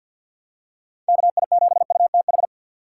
Morse code, heard as a keyed tone, about 51 wpm, 700 Hz. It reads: KI7UTH
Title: USA Call Signs - 2x Character Spacing - 50wpm